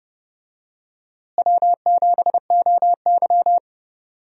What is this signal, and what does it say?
Morse code, heard as a keyed tone, about 32 wpm, 700 Hz. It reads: W7OY
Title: USA Call Signs - 4 Characters - Repeated 3x - 30wpm